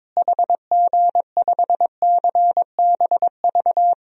Morse code, heard as a keyed tone, about 22 wpm, 700 Hz. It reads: HG5CBV
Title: DXCC-100 Call Signs - 6 Characters - Repeated 2x - 22wpm